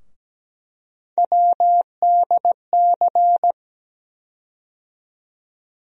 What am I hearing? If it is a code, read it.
Morse code, heard as a keyed tone, about 17 wpm, 700 Hz. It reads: WDC
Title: USA Call Sign Postfixes - 3 Letters 17wpm